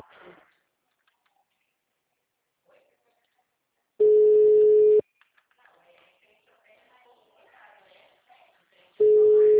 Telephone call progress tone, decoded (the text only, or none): ringback tone